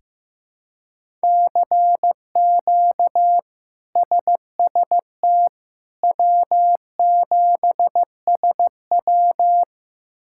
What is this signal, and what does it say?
Morse code, heard as a keyed tone, about 15 wpm, 700 Hz. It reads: CQ SST W7SW
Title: K1USN SST QSO Practice - 15wpm